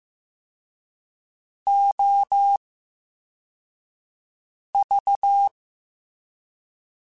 Morse code, heard as a keyed tone, about 15 wpm, spaced as for 4 wpm, 800 Hz. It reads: OV
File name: Special INGs / Special INGs-mp3s / Slow ing @13